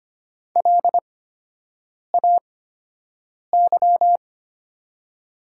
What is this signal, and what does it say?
Morse code, heard as a keyed tone, about 25 wpm, spaced as for 7 wpm, 700 Hz. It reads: LAY
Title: 8x Character Spacing - Top 500 Words - 25wpm